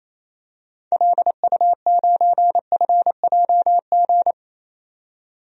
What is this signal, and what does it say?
Morse code, heard as a keyed tone, about 28 wpm, 700 Hz. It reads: LU9FJG
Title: DXCC-100 Call Signs - 6 Characters - 28wpm